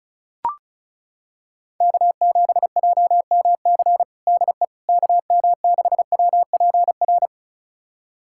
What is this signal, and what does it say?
Morse code, heard as a keyed tone, about 36 wpm, 700 Hz. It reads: K7JMC DE KM6WPR
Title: USA Callsign DE Callsign 35wpm